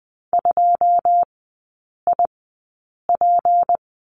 Morse code, heard as a keyed tone, about 20 wpm, spaced as for 8 wpm, 700 Hz. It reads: 2IP
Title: 152 Lesson 25e - Letter p - Three Characters 20wpm